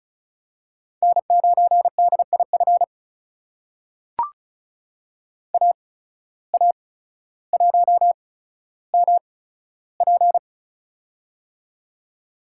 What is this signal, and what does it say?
Morse code, heard as a keyed tone, about 34 wpm, spaced as for 15 wpm, 700 Hz. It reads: N9DIF AA1MP
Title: USA Call Signs - 5 Characters - 8x Character Spacing - 35wpm